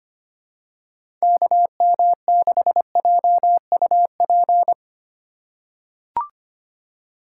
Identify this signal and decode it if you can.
Morse code, heard as a keyed tone, about 25 wpm, 700 Hz. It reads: KM6JUP
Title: USA Call Signs - 6 Characters - 4x Character Spacing - 25wpm